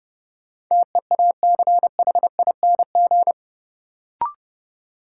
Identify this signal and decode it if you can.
Morse code, heard as a keyed tone, about 32 wpm, 700 Hz. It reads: TEACHING
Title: Sentences from Top 1100 Words - New Word Drill - 30wpm